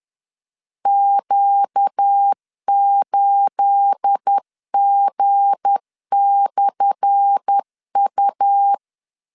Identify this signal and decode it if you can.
Morse code, heard as a keyed tone, about 10 wpm, 800 Hz. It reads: Q8G/U
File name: LEZIONI / 10 SIGNS / 10D COMA 10wpm 3013